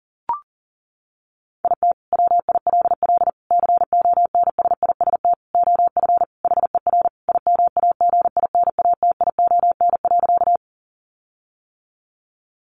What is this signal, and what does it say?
Morse code, heard as a keyed tone, about 40 wpm, 700 Hz. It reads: IT WILL CONSIST OF HER IMAGINATION.
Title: Sentences from Top 1400 Words 40wpm